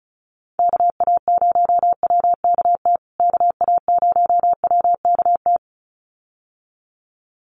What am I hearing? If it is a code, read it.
Morse code, heard as a keyed tone, about 34 wpm, 700 Hz. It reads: KA0WKT KA0WKT
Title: USA Call Signs Repeated 2x 35wpm